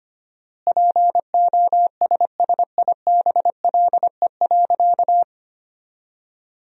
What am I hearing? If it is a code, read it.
Morse code, heard as a keyed tone, about 25 wpm, 700 Hz. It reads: POSSIBLE.
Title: 4x Word-Spacing - Sentences from Top 500 Words - 25wpm